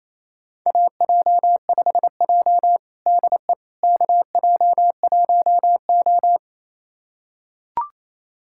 Morse code, heard as a keyed tone, about 28 wpm, 700 Hz. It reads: AJ5J DE KJ1O
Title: USA Callsign DE Callsign - 4 Characters - 28wpm